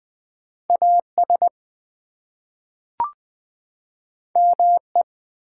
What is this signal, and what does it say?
Morse code, heard as a keyed tone, about 20 wpm, 700 Hz. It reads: AS ME
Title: Two Letter Words - 20wpm